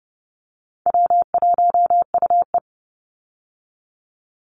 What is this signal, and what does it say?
Morse code, heard as a keyed tone, about 30 wpm, 700 Hz. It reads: W1UE
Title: Easy Call Signs Repeated 3x - 30wpm